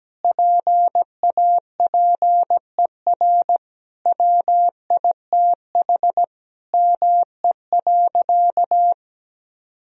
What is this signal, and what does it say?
Morse code, heard as a keyed tone, about 17 wpm, 700 Hz. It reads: PAPER WITH ME.
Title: Sentences from Top 1000 Words - Review Part 2 - 17wpm